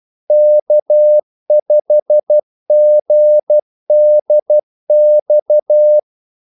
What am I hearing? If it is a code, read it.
Morse code, heard as a keyed tone, about 12 wpm, 600 Hz. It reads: K5GDX
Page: 2 ekstra ordmellemrum mellem kaldesignalerne